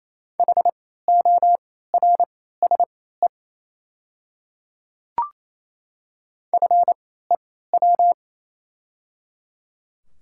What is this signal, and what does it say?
Morse code, heard as a keyed tone, about 27 wpm, spaced as for 16 wpm, 690 Hz. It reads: HORSE FEW
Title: 3x Character Spacing - Top 500 Words - 28wpm